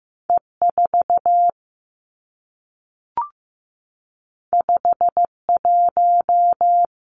Morse code, heard as a keyed tone, about 15 wpm, 700 Hz. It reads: E 4 5 1
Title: Two Challenging Letter-Number Combinations 15wpm